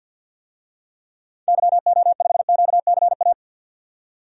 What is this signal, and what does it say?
Morse code, heard as a keyed tone, about 52 wpm, 700 Hz. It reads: YO5QCA
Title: DXCC-100 Call Signs - 6 Characters - 50wpm